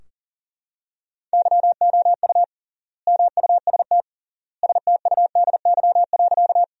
Morse code, heard as a keyed tone, about 40 wpm, 700 Hz. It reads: YOU MUST STUDY.
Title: 3x Word-Spacing - Sentences from Top 300 Words - 40wpm